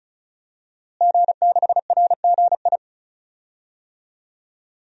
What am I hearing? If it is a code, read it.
Morse code, heard as a keyed tone, about 36 wpm, 700 Hz. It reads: G6RGI
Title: DXCC-100 Call Signs - 5 Characters - Repeated 2x - 35wpm